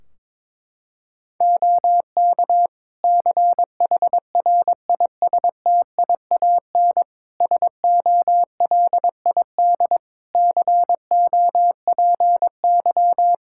Morse code, heard as a keyed tone, about 22 wpm, 700 Hz. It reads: OK CHRISTIAN SOLID COPY
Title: Practice QSO - Exchange 3 - More Info - 22wpm